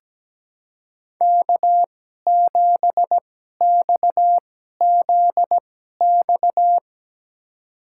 Morse code, heard as a keyed tone, about 17 wpm, spaced as for 12 wpm, 700 Hz. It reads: K7XZX
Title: USA Call Signs - 5 Characters - 2x Character Spacing - 17wpm